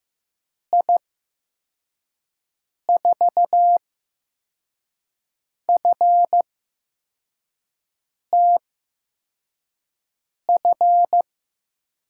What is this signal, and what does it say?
Morse code, heard as a keyed tone, about 15 wpm, spaced as for 4 wpm, 700 Hz. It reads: I4FTF
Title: USA Call Signs - 6 Characters - 8x Character Spacing - 15wpm